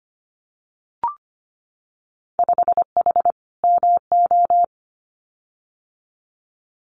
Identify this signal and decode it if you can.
Morse code, heard as a keyed tone, about 25 wpm, 700 Hz. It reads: 5H MO
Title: Winter Field Day Exchanges Encore 25wpm